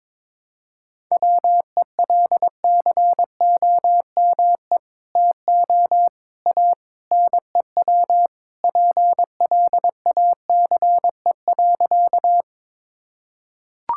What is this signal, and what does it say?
Morse code, heard as a keyed tone, about 22 wpm, 700 Hz. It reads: WELCOME TO A NEW PLACE.
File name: practice / 22wpm / Sentences with 2-Word Phrases - 22wpm